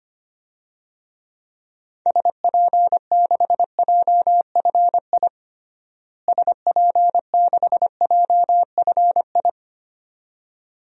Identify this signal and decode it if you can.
Morse code, heard as a keyed tone, about 25 wpm, 700 Hz. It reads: SP6JFI SP6JFI
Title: DXCC-100 Call Signs - 6 Characters - Repeated 2x - 25wpm